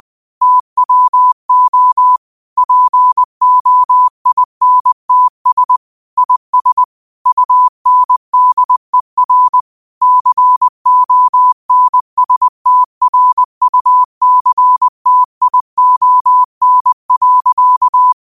Morse code, heard as a keyed tone, about 20 wpm, 1000 Hz. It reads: TWO POINTS IS UNDER CONSTRUCTION.
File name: qotdfiles / 20wpm